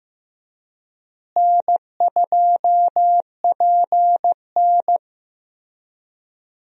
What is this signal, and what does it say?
Morse code, heard as a keyed tone, about 15 wpm, 700 Hz. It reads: N2PN
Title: USA Call Signs - 4 Characters - Repeated 3x - 15wpm